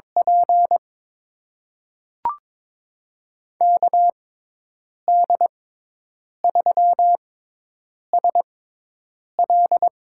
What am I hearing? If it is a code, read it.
Morse code, heard as a keyed tone, about 22 wpm, spaced as for 8 wpm, 700 Hz. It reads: P KD3SL
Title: USA Call Signs - 5 Characters - 6x Character Spacing - 22wpm